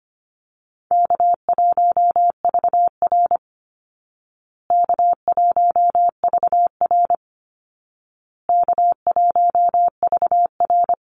Morse code, heard as a keyed tone, about 25 wpm, 700 Hz. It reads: K1VR K1VR K1VR
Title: Call Signs Repeated 2x - 25wpm